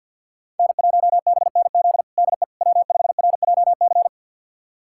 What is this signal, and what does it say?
Morse code, heard as a keyed tone, about 53 wpm, 700 Hz. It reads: N1BNZ DE W5RJC
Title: USA Callsign DE Callsign - 5 Characters - 50wpm